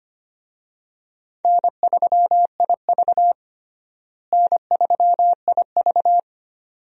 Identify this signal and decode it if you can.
Morse code, heard as a keyed tone, about 25 wpm, 700 Hz. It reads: N3IV N3IV
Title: USA Call Signs - 4 Characters - Repeated 2x - 25wpm